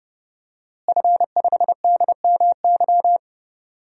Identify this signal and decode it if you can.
Morse code, heard as a keyed tone, about 30 wpm, 700 Hz. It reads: F5DMY
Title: DXCC-100 Call Signs - 5 Characters - Repeated 2x - 30wpm